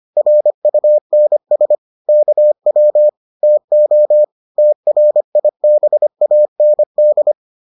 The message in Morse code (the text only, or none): RUNS KW TO TRIBAND